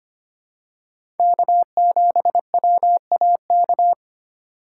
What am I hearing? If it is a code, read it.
Morse code, heard as a keyed tone, about 25 wpm, 700 Hz. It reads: K7WAK
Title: USA Call Signs - 5 Characters - 3x Character Spacing - 25wpm